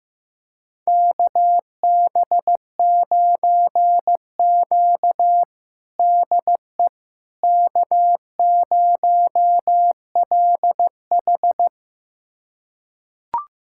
Morse code, heard as a keyed tone, about 15 wpm, 700 Hz. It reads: KB9Q DE K0LH
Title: USA Callsign DE Callsign - 4 Characters - 15wpm